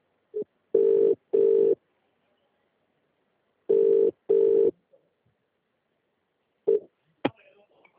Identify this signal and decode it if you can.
Ringback tone (UK), 400 + 450 Hz.